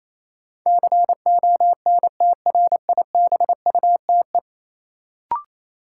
Morse code, heard as a keyed tone, about 28 wpm, 700 Hz. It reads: CONTRIBUTE
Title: Sentences from Top 1400 Words - New Word Drill Alt - 28wpm